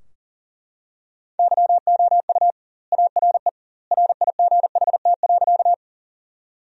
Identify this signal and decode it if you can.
Morse code, heard as a keyed tone, about 40 wpm, 700 Hz. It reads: YOU ARE RIGHT.
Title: 2x Word-Spacing - Sentences from Top 200 Words - 40wpm